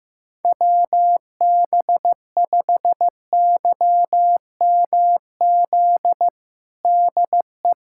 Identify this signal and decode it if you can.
Morse code, heard as a keyed tone, about 16 wpm, 700 Hz. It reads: WB5YMZ DE
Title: USA Callsign DE Callsign 15wpm